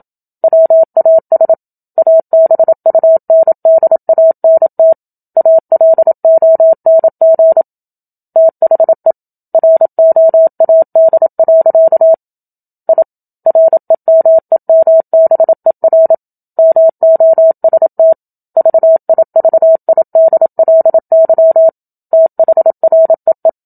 Morse code, heard as a keyed tone, about 27 wpm, 650 Hz. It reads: WAS ABUNDANT ALONG THE ROAD. I REMEMBER MOST VIVIDLY THREE
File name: Book Two / 01-Chapter 1 - 27 WPM